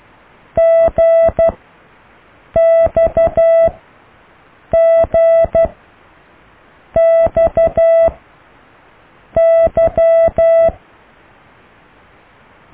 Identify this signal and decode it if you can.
Morse code, heard as a keyed tone, about 12 wpm, spaced as for 6 wpm, 660 Hz. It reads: GXGXY